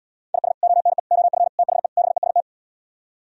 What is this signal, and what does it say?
Morse code, heard as a keyed tone, about 77 wpm, 700 Hz. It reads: IS ONE OF THE BEST